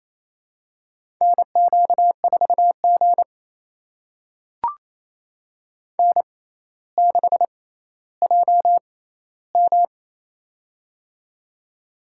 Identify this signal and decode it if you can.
Morse code, heard as a keyed tone, about 27 wpm, spaced as for 14 wpm, 700 Hz. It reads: NQ4G N6JM
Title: USA Call Signs - 4 Characters - 6x Character Spacing - 28wpm